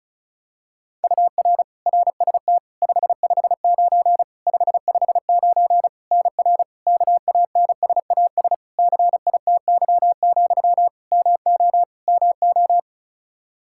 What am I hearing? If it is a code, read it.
Morse code, heard as a keyed tone, about 36 wpm, 700 Hz. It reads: UR RST 559 559 NR KANSAS CITY, MO MO
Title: K1USN SST QSO Practice 35wpm